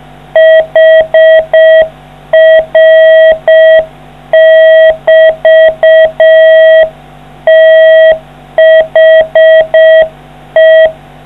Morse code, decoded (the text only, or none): HR<BT> THE